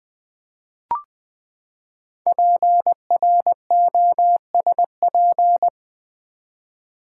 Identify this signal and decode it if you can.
Morse code, heard as a keyed tone, about 20 wpm, 700 Hz. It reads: PROSP
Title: Build A Word - 5 Letters At A Time - 10-Letter Words - 20wpm